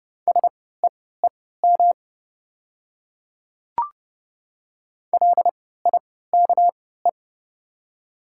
Morse code, heard as a keyed tone, about 30 wpm, spaced as for 17 wpm, 690 Hz. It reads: SEEM LIKE
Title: 3x Character Spacing - Top 300 Words - 30wpm